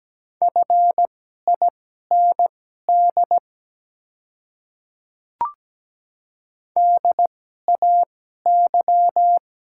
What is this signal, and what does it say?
Morse code, heard as a keyed tone, about 17 wpm, spaced as for 12 wpm, 700 Hz. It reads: FIND DAY
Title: 2x Character Spacing - Top 200 Words - 17wpm